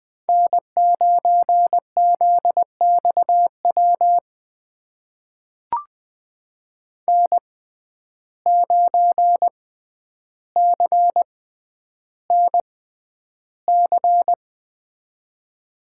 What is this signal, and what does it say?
Morse code, heard as a keyed tone, about 20 wpm, spaced as for 10 wpm, 700 Hz. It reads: N9ZXW N9CNC
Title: USA Call Signs - 5 Characters - 6x Character Spacing - 20wpm